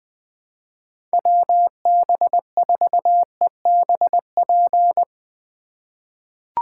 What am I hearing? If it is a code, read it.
Morse code, heard as a keyed tone, about 20 wpm, 700 Hz. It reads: WB4EBP
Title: USA Call Signs - 2x Character Spacing - 20wpm